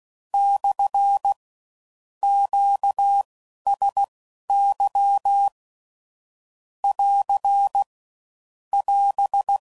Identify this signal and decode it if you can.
Morse code, heard as a keyed tone, about 16 wpm, spaced as for 12 wpm, 800 Hz. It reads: / QSY <AR> <AS>
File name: CW Code Groups @15WPM / Q Calls / Q Calls SPSL2